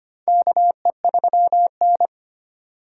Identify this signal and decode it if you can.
Morse code, heard as a keyed tone, about 25 wpm, 700 Hz. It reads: KE3N
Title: USA Call Signs - 4 Characters - 2x Character Spacing - 25wpm